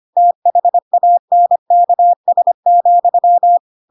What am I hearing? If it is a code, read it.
Morse code, heard as a keyed tone, about 25 wpm, 700 Hz. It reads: THANKS,
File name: Pinocchio / Pinocchio ch5 25 25 ik1hge